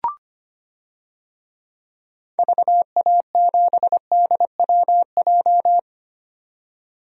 Morse code, heard as a keyed tone, about 25 wpm, 700 Hz. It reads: VA7DWJ
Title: DXCC-100 Call Signs - 6 Characters - 25wpm